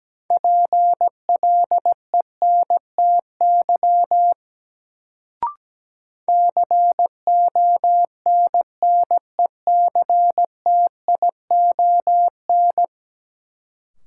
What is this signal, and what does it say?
Morse code, heard as a keyed tone, about 17 wpm, 700 Hz. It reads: PLENTY CONNECTION
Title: Sentences from Top 1300 Words - New Word Drill - 17wpm